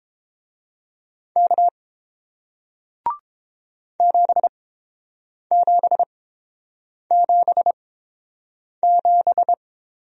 Morse code, heard as a keyed tone, about 30 wpm, spaced as for 7 wpm, 700 Hz. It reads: K 7777